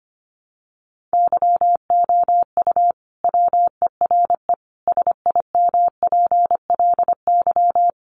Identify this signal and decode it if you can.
Morse code, heard as a keyed tone, about 25 wpm, 700 Hz. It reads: YOU WERE SIMPLY
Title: Sentences from Top 1100 Words 25wpm